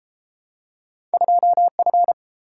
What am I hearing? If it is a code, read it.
Morse code, heard as a keyed tone, about 32 wpm, 700 Hz. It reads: 2F